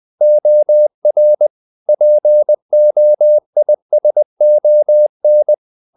Morse code, heard as a keyed tone, about 20 wpm, 600 Hz. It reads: OR POISON
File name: the old man and the sea / the old man and the sea part 1 20wpm